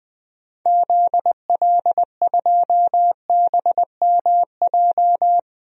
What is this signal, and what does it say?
Morse code, heard as a keyed tone, about 20 wpm, 700 Hz. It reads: ZL2BMJ
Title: DXCC-100 Call Signs - 6 Characters - Repeated 3x - 20wpm